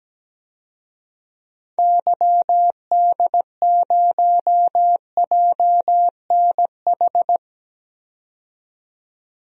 Morse code, heard as a keyed tone, about 17 wpm, 700 Hz. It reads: YD0JNH
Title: DXCC-100 Call Signs - 6 Characters - 17wpm